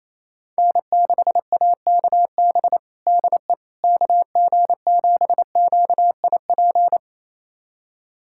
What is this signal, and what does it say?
Morse code, heard as a keyed tone, about 28 wpm, 700 Hz. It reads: N6AKB DE KG7QIP
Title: USA Callsign DE Callsign 28wpm